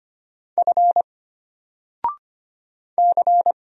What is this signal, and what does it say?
Morse code, heard as a keyed tone, about 25 wpm, 700 Hz. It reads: F C
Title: Single Character (Letter:Number:Punctuation:Prosign) Alt 25wpm